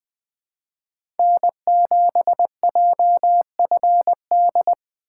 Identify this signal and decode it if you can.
Morse code, heard as a keyed tone, about 20 wpm, 700 Hz. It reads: N7JFD
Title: USA Call Signs - 5 Characters - 2x Character Spacing - 20wpm